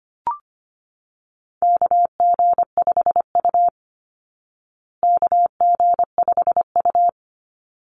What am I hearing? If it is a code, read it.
Morse code, heard as a keyed tone, about 25 wpm, 700 Hz. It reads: KG5U KG5U
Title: Call Signs Repeated 3x - 25wpm